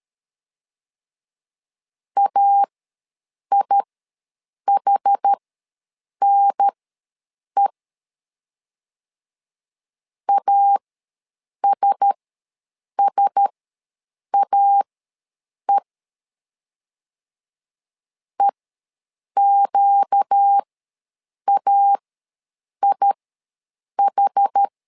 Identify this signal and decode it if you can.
Morse code, heard as a keyed tone, about 13 wpm, spaced as for 7 wpm, 800 Hz. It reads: AIHNE ASSAE EQAIH
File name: LEZIONI / 4 Q1S ws 12wpm 1827